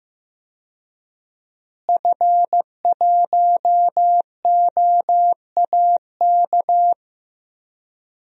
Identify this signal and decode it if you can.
Morse code, heard as a keyed tone, about 15 wpm, 700 Hz. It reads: F1OAK
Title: DXCC-100 Call Signs - 5 Characters - Repeated 3x - 15wpm